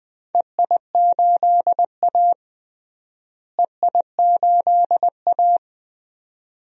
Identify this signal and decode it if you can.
Morse code, heard as a keyed tone, about 20 wpm, 700 Hz. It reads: EI8A EI8A
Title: DXCC-100 Call Signs - 4 Characters - Repeated 2x - 20wpm